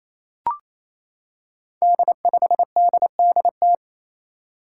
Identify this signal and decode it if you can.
Morse code, heard as a keyed tone, about 28 wpm, 700 Hz. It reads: D5DDT
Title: Five Challenging Letter-Number Combinations 28wpm